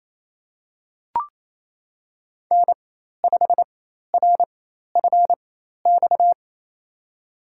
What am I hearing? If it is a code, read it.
Morse code, heard as a keyed tone, about 28 wpm, spaced as for 13 wpm, 700 Hz. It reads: N5RFX
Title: USA Call Signs - 4x Character Spacing - 28wpm